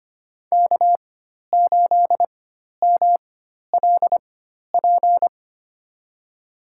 Morse code, heard as a keyed tone, about 25 wpm, spaced as for 12 wpm, 700 Hz. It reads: K8MLP
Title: USA Call Signs - 5 Characters - 4x Character Spacing - 25wpm